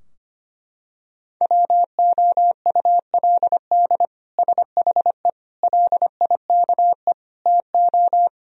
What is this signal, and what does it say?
Morse code, heard as a keyed tone, about 25 wpm, 700 Hz. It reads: WOULD SHE LIKE TO
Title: Sentences from Top 1000 Words - Review Part 2 - 25wpm